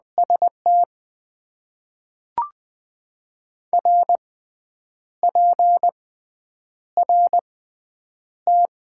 Morse code, heard as a keyed tone, about 21 wpm, spaced as for 8 wpm, 700 Hz. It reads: ST RPRT
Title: Top 90 QSO Elements - 6x Character Spacing - 20wpm